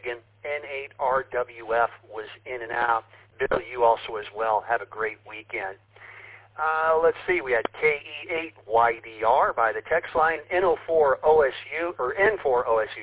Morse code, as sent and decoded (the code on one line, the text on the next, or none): none